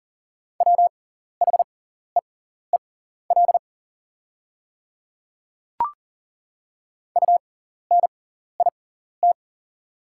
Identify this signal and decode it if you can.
Morse code, heard as a keyed tone, about 38 wpm, spaced as for 14 wpm, 700 Hz. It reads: WHEEL UNIT
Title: 6x Character Spacing - Top 500 Words - 40wpm